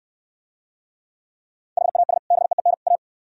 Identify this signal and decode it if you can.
Morse code, heard as a keyed tone, about 65 wpm, 700 Hz. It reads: HAS BEEN A